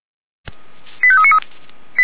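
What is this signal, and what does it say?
Five-tone selective call (ZVEI), heard as quick runs of five tones, tones of about 80 ms. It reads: 85393
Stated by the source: ZVEI five-tone selective call 89315, received with an SDR by Feuerwehr Orlamünde.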